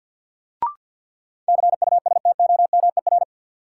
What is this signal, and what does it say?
Morse code, heard as a keyed tone, about 50 wpm, 700 Hz. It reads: CUSTOMER
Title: Sentences from Top 1100 Words - New Word Drill - 50wpm